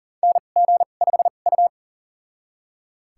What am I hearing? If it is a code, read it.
Morse code, heard as a keyed tone, about 40 wpm, spaced as for 29 wpm, 700 Hz. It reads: NG5U
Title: USA Call Signs - 4 Characters - 2x Character Spacing - 40wpm